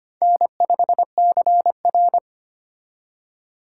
Morse code, heard as a keyed tone, about 32 wpm, 700 Hz. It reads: N5CR